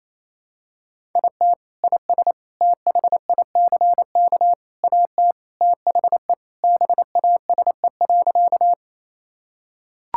Morse code, heard as a keyed tone, about 28 wpm, 700 Hz. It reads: IT IS THICK AT THE BASE.
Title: Sentences from Top 1600 Words 28wpm